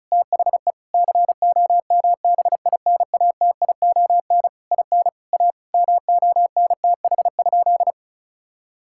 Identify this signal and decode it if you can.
Morse code, heard as a keyed tone, about 36 wpm, 700 Hz. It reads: THE COMBINATION IN A MONTH?
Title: Sentences from Top 1000 Words - Review Part 6 - 35wpm